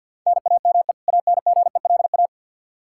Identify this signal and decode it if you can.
Morse code, heard as a keyed tone, about 51 wpm, 700 Hz. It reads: NAME ANGELA
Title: K1USN SST QSO Practice 50wpm